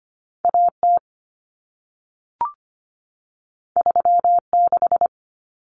Morse code, heard as a keyed tone, about 25 wpm, 700 Hz. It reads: AT 36
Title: Common Cut Numbers - 2-Digit 25wpm